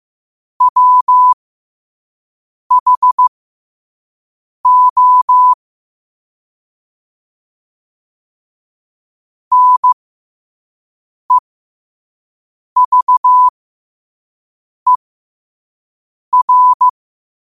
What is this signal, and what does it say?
Morse code, heard as a keyed tone, about 15 wpm, spaced as for 5 wpm, 1000 Hz. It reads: WHO NEVER
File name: qotdfiles / 5wpm